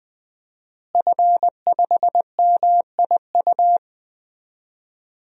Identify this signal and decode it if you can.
Morse code, heard as a keyed tone, about 20 wpm, 700 Hz. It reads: F5MIU
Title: DXCC-100 Call Signs - 5 Characters - Repeated 2x - 20wpm